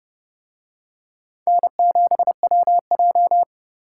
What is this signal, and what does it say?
Morse code, heard as a keyed tone, about 29 wpm, 700 Hz. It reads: N7WJ